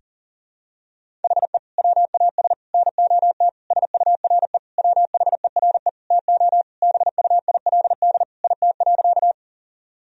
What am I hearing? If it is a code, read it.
Morse code, heard as a keyed tone, about 40 wpm, 700 Hz. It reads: HE WAS NOT SURE WHERE TO BUILD IT.
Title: Sentences from Top 1000 Words - Review Part 1 - 40wpm